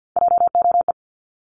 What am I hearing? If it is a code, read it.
Morse code, heard as a keyed tone, about 50 wpm, 700 Hz. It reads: JOE
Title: Multi-letter Phonemes 50wpm